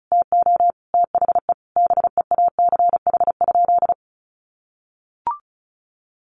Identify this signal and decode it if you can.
Morse code, heard as a keyed tone, about 34 wpm, 700 Hz. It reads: TO THE BEACH?
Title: Sentences from Top 1500 Words 35wpm